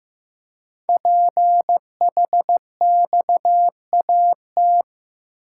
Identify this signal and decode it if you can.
Morse code, heard as a keyed tone, about 15 wpm, 700 Hz. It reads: PHXAT
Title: Code Groups - 5 Letter:Numbers - 15wpm